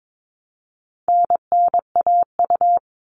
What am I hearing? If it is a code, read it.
Morse code, heard as a keyed tone, about 22 wpm, 700 Hz. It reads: NNAU
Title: All Cut Numbers - 4-Digit 22wpm